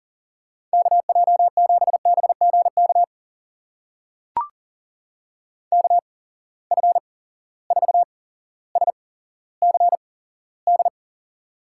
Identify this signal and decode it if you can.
Morse code, heard as a keyed tone, about 42 wpm, spaced as for 17 wpm, 700 Hz. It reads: KJ7BGK KF4SCD
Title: USA Call Signs - 6 Characters - 8x Character Spacing - 40wpm